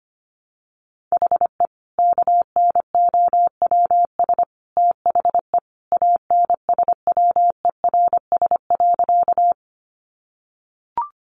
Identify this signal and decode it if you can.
Morse code, heard as a keyed tone, about 25 wpm, 700 Hz. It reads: HE KNOWS THE ANSWERS.
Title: Additional Easy Phrases 25wpm